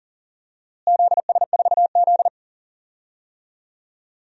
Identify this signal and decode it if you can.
Morse code, heard as a keyed tone, about 40 wpm, 700 Hz. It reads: ZS4Z
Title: DXCC-100 Call Signs - 4 Characters - Repeated 3x - 40wpm